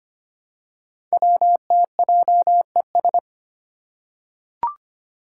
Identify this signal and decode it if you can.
Morse code, heard as a keyed tone, about 25 wpm, 700 Hz. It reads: WTJES
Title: Code Groups - 5 Letters - 25wpm